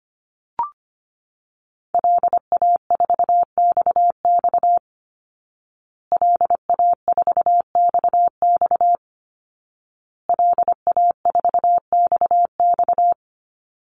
Morse code, heard as a keyed tone, about 25 wpm, 700 Hz. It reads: LA4XX LA4XX LA4XX
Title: Call Signs Repeated 2x - 25wpm